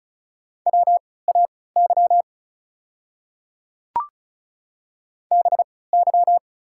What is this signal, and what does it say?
Morse code, heard as a keyed tone, about 34 wpm, spaced as for 20 wpm, 690 Hz. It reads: WAY BY
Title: 3x Character Spacing - Top 100 Words - 35wpm